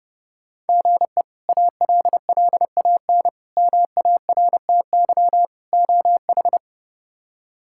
Morse code, heard as a keyed tone, about 32 wpm, 700 Hz. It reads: GE ALLAN MARTY OH
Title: K1USN SST QSO Practice - 30wpm